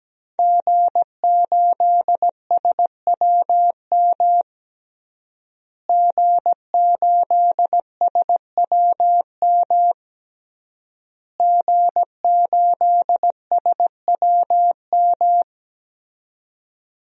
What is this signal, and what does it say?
Morse code, heard as a keyed tone, about 17 wpm, 700 Hz. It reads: G8SWM G8SWM G8SWM
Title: DXCC-100 Call Signs - 5 Characters - Repeated 3x - 17wpm